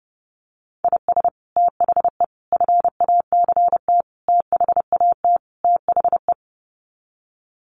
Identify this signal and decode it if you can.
Morse code, heard as a keyed tone, about 30 wpm, 700 Hz. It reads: IS THE FACT THAT THE
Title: Sets of 5 Words Encore2 30wpm